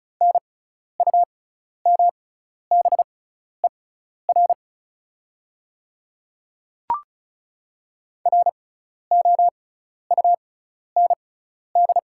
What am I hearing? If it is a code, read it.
Morse code, heard as a keyed tone, about 36 wpm, spaced as for 12 wpm, 690 Hz. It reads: NUMBER ROUND
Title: 6x Character Spacing - Top 200 Words - 35wpm